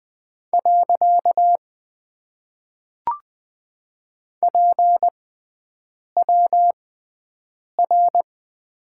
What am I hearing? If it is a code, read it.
Morse code, heard as a keyed tone, about 20 wpm, spaced as for 7 wpm, 700 Hz. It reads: . PWR
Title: Top 90 QSO Elements - 6x Character Spacing - 20wpm